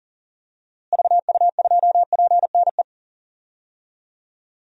Morse code, heard as a keyed tone, about 40 wpm, 700 Hz. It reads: VU2PNE
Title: DXCC-100 Call Signs - 6 Characters - Repeated 2x - 40wpm